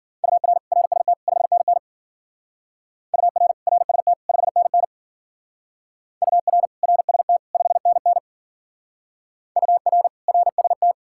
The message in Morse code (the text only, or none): UR RST 5NN UR RST 5NN UR RST 5NN UR RST